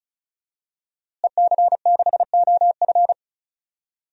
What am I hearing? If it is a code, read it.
Morse code, heard as a keyed tone, about 34 wpm, 700 Hz. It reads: EC6OF
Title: DXCC-100 Call Signs - 5 Characters - Repeated 2x - 35wpm